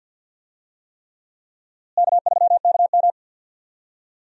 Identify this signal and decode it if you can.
Morse code, heard as a keyed tone, about 53 wpm, 700 Hz. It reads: K3KM
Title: USA Call Signs - 4 Characters - Repeated 3x - 50wpm